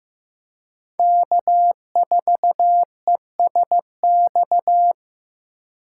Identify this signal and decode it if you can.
Morse code, heard as a keyed tone, about 15 wpm, 700 Hz. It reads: K4ESX
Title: USA Call Signs - 5 Characters - 2x Character Spacing - 15wpm